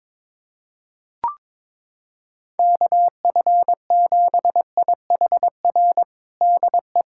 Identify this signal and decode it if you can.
Morse code, heard as a keyed tone, about 22 wpm, 700 Hz. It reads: KF7IHR DE
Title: USA Callsign DE Callsign 22wpm